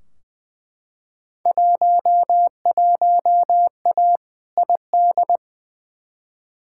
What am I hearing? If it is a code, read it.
Morse code, heard as a keyed tone, about 20 wpm, 700 Hz. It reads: 11A ID
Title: Field Day Exchanges Encore - 20wpm